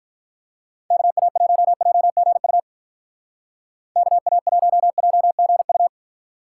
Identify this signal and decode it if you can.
Morse code, heard as a keyed tone, about 45 wpm, 700 Hz. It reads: KA1JGU KA1JGU